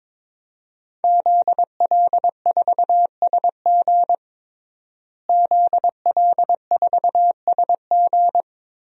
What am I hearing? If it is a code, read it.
Morse code, heard as a keyed tone, about 22 wpm, 700 Hz. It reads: ZL4SG ZL4SG
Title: DXCC-100 Call Signs - 5 Characters - Repeated 3x - 22wpm